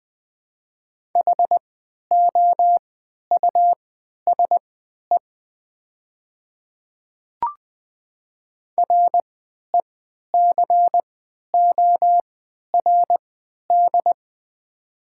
Morse code, heard as a keyed tone, about 20 wpm, spaced as for 11 wpm, 700 Hz. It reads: HOUSE RECORD
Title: 3x Character Spacing - Top 500 Words - 20wpm